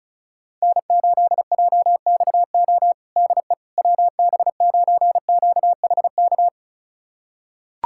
Morse code, heard as a keyed tone, about 36 wpm, 700 Hz. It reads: N8JXO DE WB9QHK
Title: USA Callsign DE Callsign 35wpm